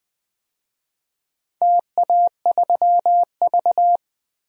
Morse code, heard as a keyed tone, about 20 wpm, 700 Hz. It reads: TA3V
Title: DXCC-100 Call Signs - 4 Characters - 20wpm